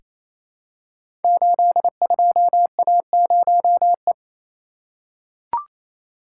Morse code, heard as a keyed tone, about 28 wpm, 700 Hz. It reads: 82A0E
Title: Code Groups - 5 Letter:Numbers - 28wpm